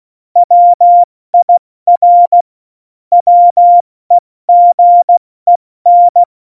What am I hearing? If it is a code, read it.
Morse code, heard as a keyed tone, about 16 wpm, 700 Hz. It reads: WIR WEGEN
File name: Morsekurs / Witze / Witz 05. Januar 2021, 70 F80